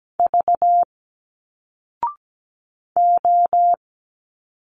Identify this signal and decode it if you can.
Morse code, heard as a keyed tone, about 17 wpm, 700 Hz. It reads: V O
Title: Single Letters 17wpm